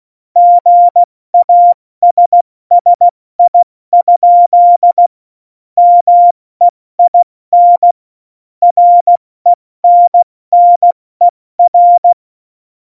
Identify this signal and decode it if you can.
Morse code, heard as a keyed tone, about 16 wpm, spaced as for 14 wpm, 700 Hz. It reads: GASSI? MEIN RENNER
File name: Morsekurs / Witze / Witz 04 August 2020, 70 F80